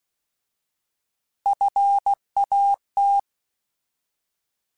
Morse code, heard as a keyed tone, about 16 wpm, 800 Hz. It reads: FAT
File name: Alphabetical / Alphabetical mp3s / Fs @15